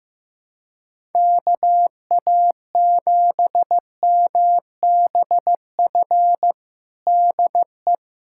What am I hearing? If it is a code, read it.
Morse code, heard as a keyed tone, about 15 wpm, 700 Hz. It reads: KA7MBF DE
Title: USA Callsign DE Callsign - 6 Characters - 15wpm